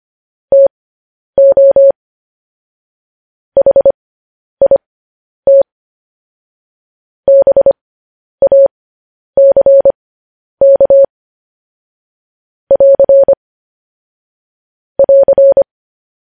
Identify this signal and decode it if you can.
Morse code, heard as a keyed tone, about 25 wpm, spaced as for 10 wpm, 550 Hz. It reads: TO HIT BACK <AR> <AR>
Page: news headlines in morse code at 10 wpm